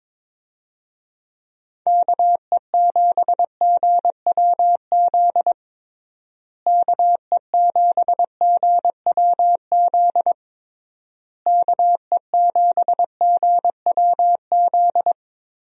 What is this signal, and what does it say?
Morse code, heard as a keyed tone, about 22 wpm, 700 Hz. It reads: KE7GWZ KE7GWZ KE7GWZ
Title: USA Call Signs - 6 Characters - Repeated 3x - 22wpm